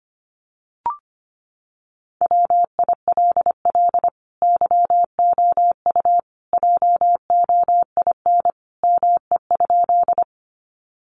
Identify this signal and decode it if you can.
Morse code, heard as a keyed tone, about 25 wpm, 700 Hz. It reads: WILL YOU JOIN ME?
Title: Sentences from Top 1400 Words 25wpm